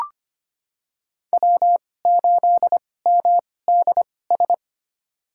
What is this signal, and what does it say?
Morse code, heard as a keyed tone, about 25 wpm, spaced as for 18 wpm, 700 Hz. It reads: W8MDS
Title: USA Call Signs - 2x Character Spacing - 25wpm